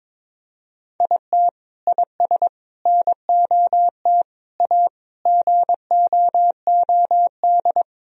Morse code, heard as a keyed tone, about 22 wpm, 700 Hz. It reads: IT IS NOT A GOOD
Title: Sentences from Top 1000 Words - Review Part 2 - 22wpm